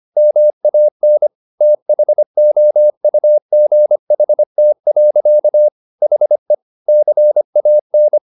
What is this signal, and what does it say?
Morse code, heard as a keyed tone, about 25 wpm, 600 Hz. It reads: MAN THOUGHT. HE CAN
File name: the old man and the sea part 1 25wpm